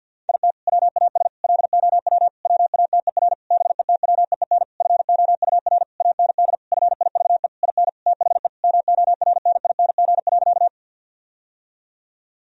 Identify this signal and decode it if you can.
Morse code, heard as a keyed tone, about 53 wpm, 700 Hz. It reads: IT WAS LOW WATER BETWEEN FOUR AND FIVE IN THE MORNING.
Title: Sentences from Top 1000 Words - Review Part 2 - 50wpm